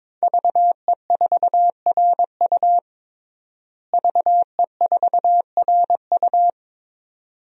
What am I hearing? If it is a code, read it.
Morse code, heard as a keyed tone, about 22 wpm, 700 Hz. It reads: VE4RU VE4RU
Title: DXCC-100 Call Signs - 5 Characters - Repeated 2x - 22wpm